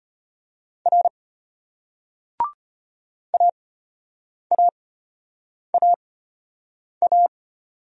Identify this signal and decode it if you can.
Morse code, heard as a keyed tone, about 34 wpm, spaced as for 8 wpm, 700 Hz. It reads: R AAAA